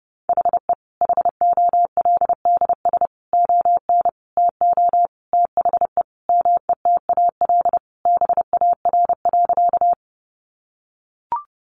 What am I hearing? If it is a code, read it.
Morse code, heard as a keyed tone, about 28 wpm, 700 Hz. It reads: HE HOLDS ON TO THE METAL BAR.
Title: Additional Easy Phrases 30wpm